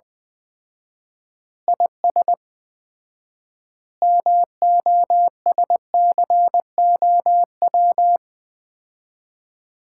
Morse code, heard as a keyed tone, about 21 wpm, 700 Hz. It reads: IS MOSCOW
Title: Common QSO Phrases - 4x Word-Spacing - 20wpm